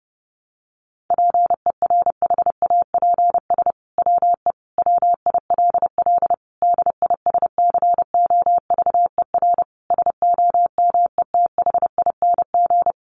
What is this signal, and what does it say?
Morse code, heard as a keyed tone, about 30 wpm, 700 Hz. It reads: PERHAPS WE WILL DISCOVER SOMETHING
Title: Sentences from Top 1100 Words 30wpm